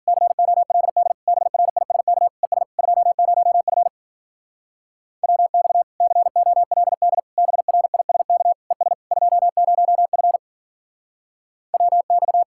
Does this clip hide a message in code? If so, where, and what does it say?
Yes: Morse code from 0.1 to 12.4 s, heard as a keyed tone, about 50 wpm, 700 Hz. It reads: COLD BRISK ES 20F WX COLD BRISK ES 20F WX